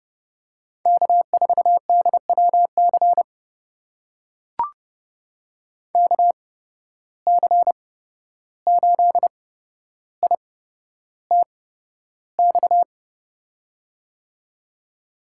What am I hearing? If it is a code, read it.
Morse code, heard as a keyed tone, about 30 wpm, spaced as for 12 wpm, 700 Hz. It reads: K4DWC KC8ITX
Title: USA Call Signs - 8x Character Spacing - 30wpm